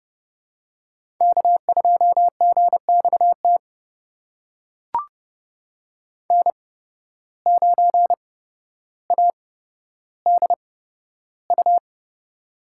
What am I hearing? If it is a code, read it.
Morse code, heard as a keyed tone, about 31 wpm, spaced as for 13 wpm, 700 Hz. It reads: K2GXT N9ADU
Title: USA Call Signs - 5 Characters - 8x Character Spacing - 30wpm